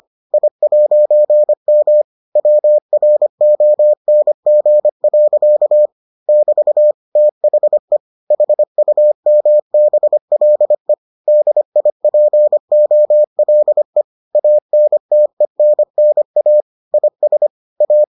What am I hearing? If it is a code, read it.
Morse code, heard as a keyed tone, about 25 wpm, 600 Hz. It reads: I'M WRONG. <BT> THE HUMBLE DIPOLE ANTENNA IS A